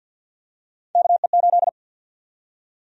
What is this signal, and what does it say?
Morse code, heard as a keyed tone, about 53 wpm, 700 Hz. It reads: KE8
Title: USA Call Sign Prefixes - Repeated 2x - 50wpm